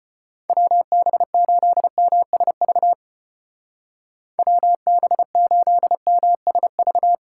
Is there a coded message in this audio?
Yes: Morse code from 0.5 to 7.2 s, heard as a keyed tone, about 33 wpm, 700 Hz. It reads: WB8MSV WB8MSV